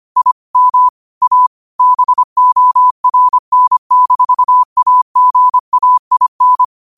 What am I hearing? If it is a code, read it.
Morse code, heard as a keyed tone, about 25 wpm, 1000 Hz. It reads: IM A BORN-AGAIN
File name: qotdfiles / 25wpm